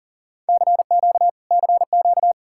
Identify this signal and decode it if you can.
Morse code, heard as a keyed tone, about 60 wpm, spaced as for 50 wpm, 700 Hz. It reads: CQ CQ
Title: Speed-Racing Common QSO Phrases Encore - 40wpm